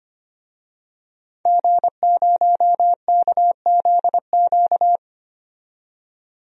Morse code, heard as a keyed tone, about 25 wpm, 700 Hz. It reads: G0KZQ
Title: DXCC-100 Call Signs - 5 Characters - 25wpm